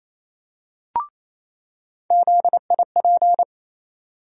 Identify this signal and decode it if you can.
Morse code, heard as a keyed tone, about 28 wpm, 700 Hz. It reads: ZIP
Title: USA Call Sign Postfixes - 3 Letters 28wpm